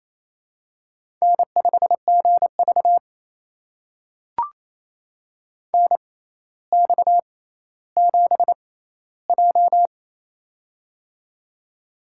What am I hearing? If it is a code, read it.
Morse code, heard as a keyed tone, about 28 wpm, spaced as for 14 wpm, 700 Hz. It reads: N5GV NX7J
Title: USA Call Signs - 6x Character Spacing - 28wpm